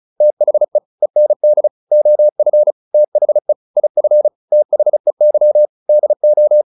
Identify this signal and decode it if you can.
Morse code, heard as a keyed tone, about 34 wpm, 600 Hz. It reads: THE END OF THE IF THEY DO